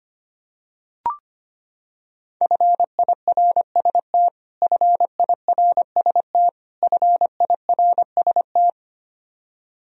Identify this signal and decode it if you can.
Morse code, heard as a keyed tone, about 25 wpm, 700 Hz. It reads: FIRST FIRST FIRST
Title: Top 100 Words Repeated 25wpm